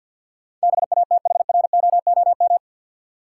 Morse code, heard as a keyed tone, about 52 wpm, 700 Hz. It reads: BATHROOM
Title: Sets of 5 Words Encore - New Word Drill - 50wpm